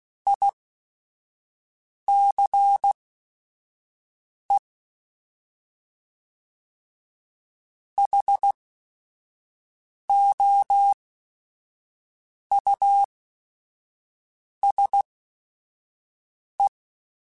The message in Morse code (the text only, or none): ICE HOUSE